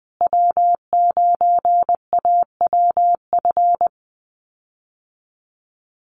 Morse code, heard as a keyed tone, about 20 wpm, 700 Hz. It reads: W9AWF
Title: USA Call Signs Repeated 2x 20wpm